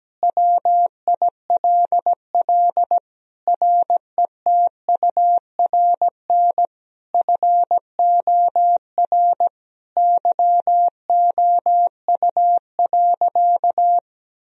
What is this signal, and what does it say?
Morse code, heard as a keyed tone, about 17 wpm, 700 Hz. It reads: WILL RETURN FOR YOU.
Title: Sentences from Top 1000 Words - Review Part 1 - 17wpm